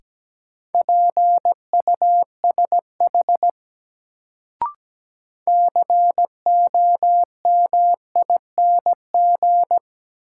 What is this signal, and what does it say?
Morse code, heard as a keyed tone, about 17 wpm, 700 Hz. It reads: PUSH COMING
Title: Sentences from Top 1300 Words - New Word Drill - 17wpm